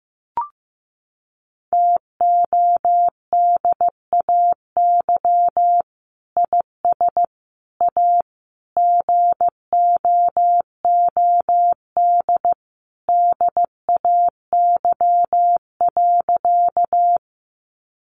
Morse code, heard as a keyed tone, about 15 wpm, 700 Hz. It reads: TODAY IS A GOOD DAY.
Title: Sentences from Top 600 Words 15wpm